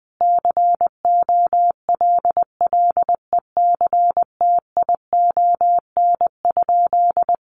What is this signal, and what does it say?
Morse code, heard as a keyed tone, about 20 wpm, 700 Hz. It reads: COLLECTION?
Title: Sentences from Top 1200 Words 20wpm